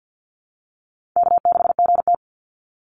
Morse code, heard as a keyed tone, about 50 wpm, 700 Hz. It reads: K6GT
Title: Call Signs - 50wpm